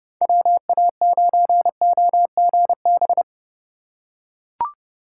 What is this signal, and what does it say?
Morse code, heard as a keyed tone, about 28 wpm, 700 Hz. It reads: WA9OGB